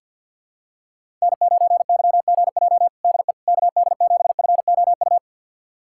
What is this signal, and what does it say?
Morse code, heard as a keyed tone, about 52 wpm, 700 Hz. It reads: N9YGJ DE KD7FOU
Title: USA Callsign DE Callsign 50wpm